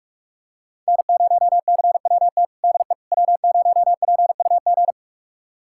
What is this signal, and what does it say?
Morse code, heard as a keyed tone, about 44 wpm, 700 Hz. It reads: N0CWT DE W0PUG
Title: USA Callsign DE Callsign - 5 Characters - 45wpm